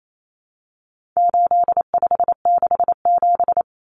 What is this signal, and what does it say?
Morse code, heard as a keyed tone, about 28 wpm, 700 Hz. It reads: 8567
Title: Numbers - 4-Digit 28wpm